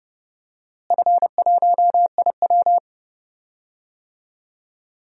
Morse code, heard as a keyed tone, about 30 wpm, 700 Hz. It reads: F1IW
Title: DXCC-100 Call Signs - 4 Characters - Repeated 3x - 30wpm